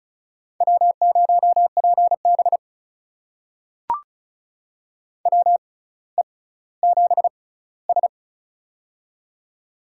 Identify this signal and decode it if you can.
Morse code, heard as a keyed tone, about 36 wpm, spaced as for 18 wpm, 700 Hz. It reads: W0PB WE7S
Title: USA Call Signs - 4 Characters - 6x Character Spacing - 35wpm